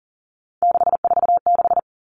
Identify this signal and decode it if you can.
Morse code, heard as a keyed tone, about 40 wpm, 700 Hz. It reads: 646
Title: Numbers - 3-Digit 40wpm